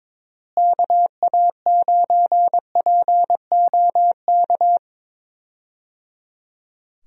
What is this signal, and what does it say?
Morse code, heard as a keyed tone, about 22 wpm, 700 Hz. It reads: KA9POK
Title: USA Call Signs - 6 Characters - Repeated 3x - 22wpm